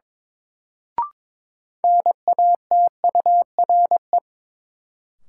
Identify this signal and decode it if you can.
Morse code, heard as a keyed tone, about 23 wpm, 700 Hz. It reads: NATURE
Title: Sentences from Top 600 Words - New Word Drill - 22wpm